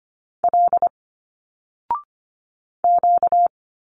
Morse code, heard as a keyed tone, about 25 wpm, 700 Hz. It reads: L Q
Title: Single Letters 25wpm